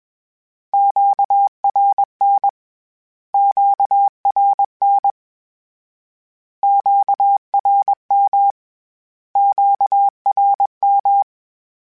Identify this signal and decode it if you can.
Morse code, heard as a keyed tone, about 21 wpm, 800 Hz. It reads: QRN QRN QRM QRM
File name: Q Calls / Q Calls-mp3s / Q Calls Twice @20